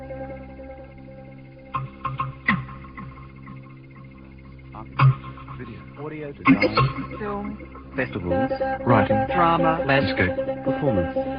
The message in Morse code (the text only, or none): none